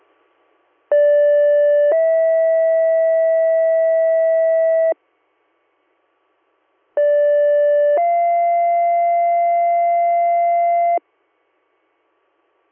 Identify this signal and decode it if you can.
Two-tone page, heard as a one-second tone, then a three-second tone, about 580 then 650 Hz, about 580 then 690 Hz. It is heard from 0.9 to 11.0 s.